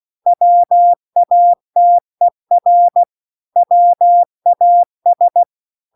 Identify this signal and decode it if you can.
Morse code, heard as a keyed tone, about 16 wpm, 700 Hz. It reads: WATER WAS
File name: CW-The-Big-Twoheaded-River-2-16WPM